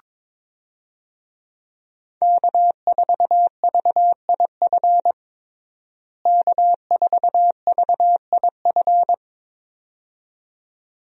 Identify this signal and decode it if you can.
Morse code, heard as a keyed tone, about 21 wpm, 700 Hz. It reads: K4VIF K4VIF
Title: USA Call Signs - 5 Characters - Repeated 2x - 22wpm